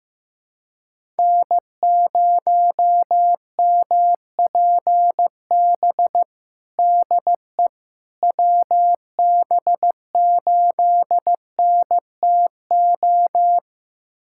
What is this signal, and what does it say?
Morse code, heard as a keyed tone, about 15 wpm, 700 Hz. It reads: N0MPB DE WB8NTO
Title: USA Callsign DE Callsign 15wpm